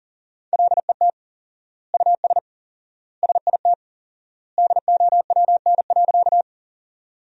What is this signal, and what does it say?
Morse code, heard as a keyed tone, about 40 wpm, 700 Hz. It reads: LET US SIT DOWN.
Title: 4x Word-Spacing - Sentences from Top 500 Words - 40wpm